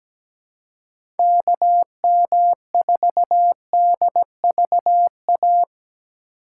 Morse code, heard as a keyed tone, about 17 wpm, 700 Hz. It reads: KM4DVA
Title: USA Call Signs - 6 Characters - 2x Character Spacing - 17wpm